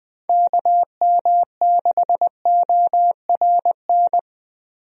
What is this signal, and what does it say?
Morse code, heard as a keyed tone, about 21 wpm, 700 Hz. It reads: KM6ORN
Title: USA Call Signs - 6 Characters - Repeated 2x - 20wpm